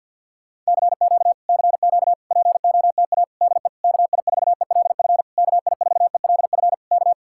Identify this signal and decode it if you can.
Morse code, heard as a keyed tone, about 62 wpm, 700 Hz. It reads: CQ CQ POTA DE KI4ELF KI4ELF K
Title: Speed-Racing Common QSO Phrases Encore - 50wpm